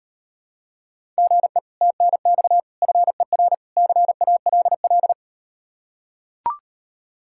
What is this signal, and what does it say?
Morse code, heard as a keyed tone, about 38 wpm, 700 Hz. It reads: GE TNX FER CALL